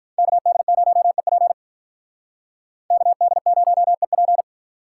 Morse code, heard as a keyed tone, about 45 wpm, 700 Hz. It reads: KD0EP KD0EP